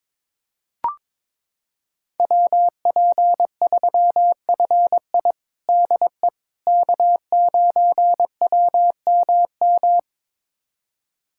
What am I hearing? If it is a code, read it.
Morse code, heard as a keyed tone, about 22 wpm, 700 Hz. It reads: WP3FI DE K9WMM
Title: USA Callsign DE Callsign - 5 Characters - 22wpm